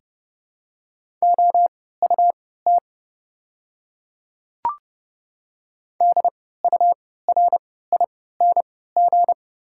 Morse code, heard as a keyed tone, about 31 wpm, spaced as for 17 wpm, 700 Hz. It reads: OUT DURING
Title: 3x Character Spacing - Top 500 Words - 30wpm